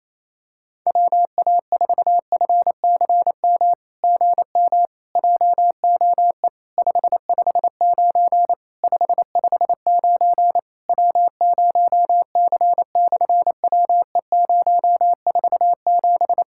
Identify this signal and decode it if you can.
Morse code, heard as a keyed tone, about 28 wpm, 700 Hz. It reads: WA4FCM GM JOE 559 559 W0C/WE047
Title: SOTA QSO Practice 28wpm